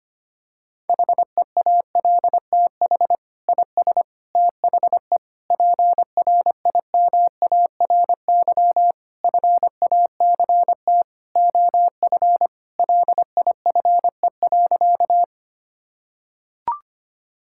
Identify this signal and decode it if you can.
Morse code, heard as a keyed tone, about 25 wpm, 700 Hz. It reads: HEALTH IS THE PRIMARY FACT OF LIFE.
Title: Sentences from Top 1000 Words - Review Part 5 - 25wpm